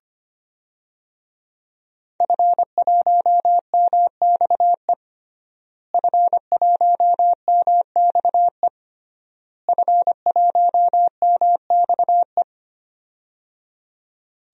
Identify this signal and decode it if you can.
Morse code, heard as a keyed tone, about 25 wpm, 700 Hz. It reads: F1MXE F1MXE F1MXE
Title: DXCC-100 Call Signs - 5 Characters - Repeated 3x - 25wpm